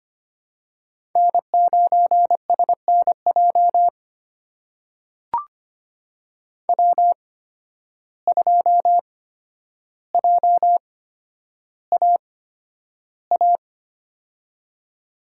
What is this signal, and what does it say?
Morse code, heard as a keyed tone, about 25 wpm, spaced as for 11 wpm, 700 Hz. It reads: N9SNJ W2JAA
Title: USA Call Signs - 8x Character Spacing - 25wpm